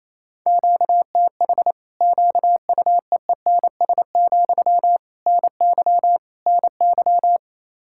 Morse code, heard as a keyed tone, about 28 wpm, 700 Hz. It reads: QTH QUEENS, NY NY
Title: Practice QSO - Exchange 1 - Beginning - 28wpm